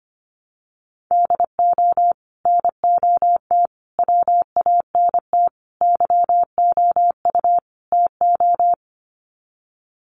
Morse code, heard as a keyed tone, about 25 wpm, 700 Hz. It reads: DO NOT WANT YOU TO
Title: Sets of 5 Words 25wpm